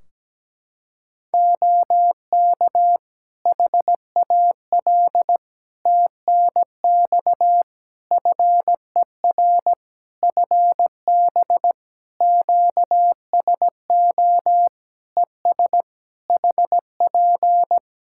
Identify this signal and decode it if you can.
Morse code, heard as a keyed tone, about 17 wpm, 700 Hz. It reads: OK HAL TNX FER FB QSO ES HP
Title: Practice QSO - Exchange 4 - Ending - 17wpm